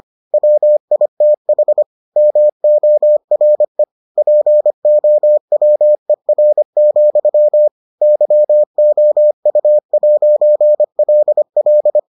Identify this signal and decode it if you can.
Morse code, heard as a keyed tone, about 25 wpm, 600 Hz. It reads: WITH MORE POWER, YOU'LL